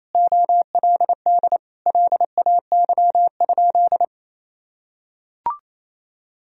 Morse code, heard as a keyed tone, about 28 wpm, 700 Hz. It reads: OLD LAY?
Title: Sentences from Top 400 Words and Abbreviations 28wpm